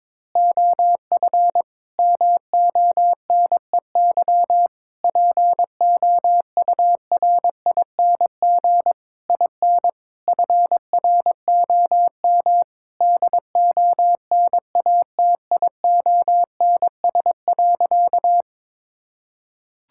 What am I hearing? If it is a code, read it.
Morse code, heard as a keyed tone, about 22 wpm, 700 Hz. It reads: OF MONEY POURING IN FROM DONATIONS.
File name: Sentences with 9-Word Phrases - 22wpm